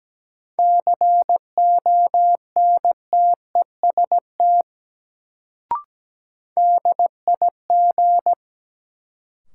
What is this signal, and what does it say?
Morse code, heard as a keyed tone, about 17 wpm, 700 Hz. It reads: CONTEST DIG
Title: Sentences from Top 1600 Words - New Word Drill - 17wpm